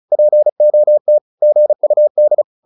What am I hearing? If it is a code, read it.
Morse code, heard as a keyed tone, about 34 wpm, 600 Hz. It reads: POT GUD